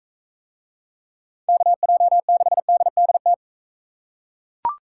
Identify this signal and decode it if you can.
Morse code, heard as a keyed tone, about 39 wpm, 700 Hz. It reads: KJ6DDT